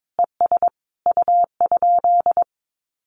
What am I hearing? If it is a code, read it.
Morse code, heard as a keyed tone, about 22 wpm, 700 Hz. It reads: ES U?
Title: Common QSO Phrases Encore 22wpm